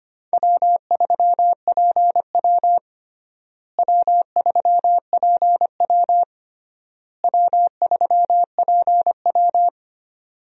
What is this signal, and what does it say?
Morse code, heard as a keyed tone, about 25 wpm, 700 Hz. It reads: W3PW W3PW W3PW
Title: USA Call Signs - 4 Characters - Repeated 3x - 25wpm